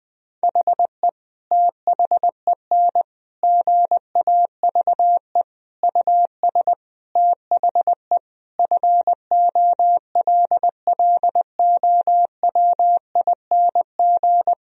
Morse code, heard as a keyed tone, about 20 wpm, 700 Hz. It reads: HE THEN GAVE US THE FOLLOWING
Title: Sentences from Top 1000 Words - Review Part 4 - 20wpm